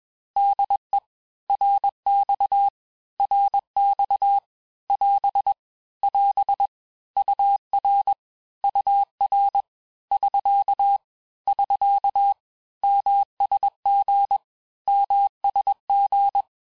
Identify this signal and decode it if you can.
Morse code, heard as a keyed tone, about 21 wpm, 800 Hz. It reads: DE RX RX <AS> <AS> UR UR <SK> <SK> MSG MSG
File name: cours-du-lundi files / Cours-1586-texte-abreviations-vitesse-15mpm-puis-20mpm